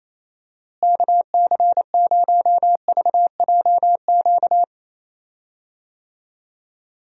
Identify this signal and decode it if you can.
Morse code, heard as a keyed tone, about 27 wpm, 700 Hz. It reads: KC0VJQ
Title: USA Call Signs - 6 Characters - Repeated 3x - 28wpm